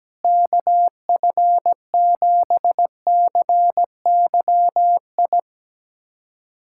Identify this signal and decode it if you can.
Morse code, heard as a keyed tone, about 17 wpm, 700 Hz. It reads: KF7CYI
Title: USA Callsign DE Callsign - 6 Characters - 17wpm